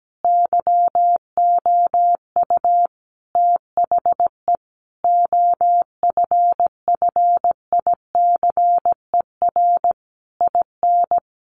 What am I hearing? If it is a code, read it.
Morse code, heard as a keyed tone, about 17 wpm, 700 Hz. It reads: YOU THE OFFICER IN
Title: Sentences from Top 1200 Words 17wpm